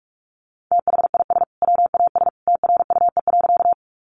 Morse code, heard as a keyed tone, about 44 wpm, 700 Hz. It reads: THIS WAS TRUE.
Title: Sentences from Top 400 Words 45wpm